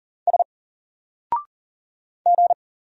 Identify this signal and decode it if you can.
Morse code, heard as a keyed tone, about 39 wpm, 700 Hz. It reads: S G
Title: Single Character (Letter:Number:Punctuation:Prosign) Alt 40wpm